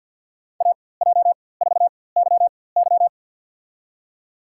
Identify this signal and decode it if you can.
Morse code, heard as a keyed tone, about 49 wpm, spaced as for 23 wpm, 700 Hz. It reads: AJ4YY
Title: USA Call Signs - 5 Characters - 4x Character Spacing - 50wpm